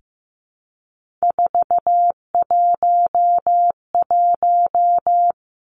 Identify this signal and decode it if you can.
Morse code, heard as a keyed tone, about 15 wpm, 700 Hz. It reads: 411
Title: Three Challenging Letter-Number Combinations 15wpm